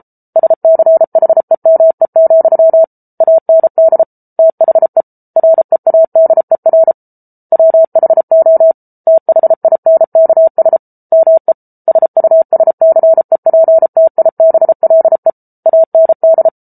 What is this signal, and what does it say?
Morse code, heard as a keyed tone, about 34 wpm, 650 Hz. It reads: SCHEME, AND THE READER WHO THINKS ME SUSCEPTIBLE AND